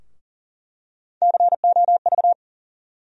Morse code, heard as a keyed tone, about 40 wpm, 700 Hz. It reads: COV
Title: USA Call Sign Postfixes - 3 Letters 40wpm